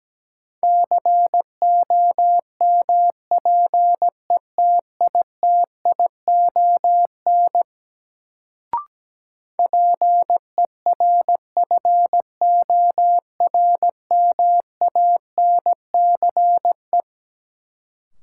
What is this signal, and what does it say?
Morse code, heard as a keyed tone, about 17 wpm, 700 Hz. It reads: COMPETITION PERFORMANCE
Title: Sentences from Top 1000 Words - New Word Drill - 17wpm